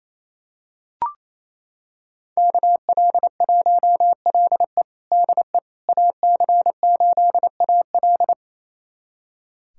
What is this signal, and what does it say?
Morse code, heard as a keyed tone, about 28 wpm, 690 Hz. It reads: KL1LE DE AC8AL
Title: USA Callsign DE Callsign - 5 Characters - 28wpm